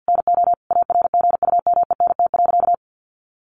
Morse code, heard as a keyed tone, about 50 wpm, 700 Hz. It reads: NO ARGUMENT.
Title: A Sound Of Thunder 50wpm